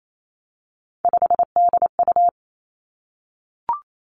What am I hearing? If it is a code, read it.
Morse code, heard as a keyed tone, about 28 wpm, 700 Hz. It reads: 5DU
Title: Three Challenging Letter-Number Combinations 28wpm